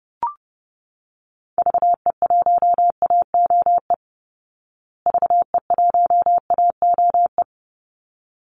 Morse code, heard as a keyed tone, about 30 wpm, 700 Hz. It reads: VE1AOE VE1AOE
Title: Call Signs Repeated 3x - 30wpm